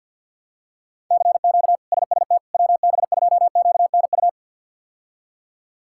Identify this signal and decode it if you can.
Morse code, heard as a keyed tone, about 51 wpm, 700 Hz. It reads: CQ SST WB2QNU
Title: K1USN SST QSO Practice - 50wpm